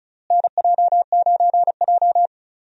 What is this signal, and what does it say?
Morse code, heard as a keyed tone, about 36 wpm, 700 Hz. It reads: NJ9J
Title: USA Call Signs - 4 Characters - Repeated 2x - 35wpm